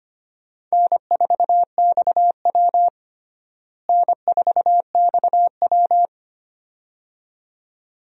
Morse code, heard as a keyed tone, about 25 wpm, 700 Hz. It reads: N4XW N4XW
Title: USA Call Signs - 4 Characters - Repeated 2x - 25wpm